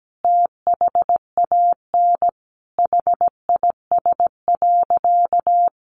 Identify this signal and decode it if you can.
Morse code, heard as a keyed tone, about 17 wpm, 700 Hz. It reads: THAN HIS.
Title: Sentences from Top 1500 Words 17wpm